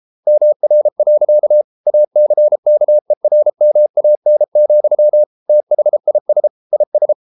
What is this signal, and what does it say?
Morse code, heard as a keyed tone, about 33 wpm, 600 Hz. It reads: MR. ACKERMAN, THIS IS